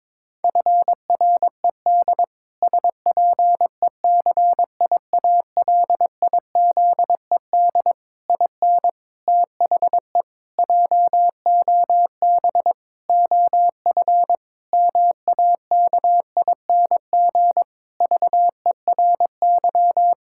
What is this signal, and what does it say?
Morse code, heard as a keyed tone, about 22 wpm, 700 Hz. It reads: FRED SPECIALIZED IN THE JOB OF MAKING VERY
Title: Pangram Challenge v2 - 22wpm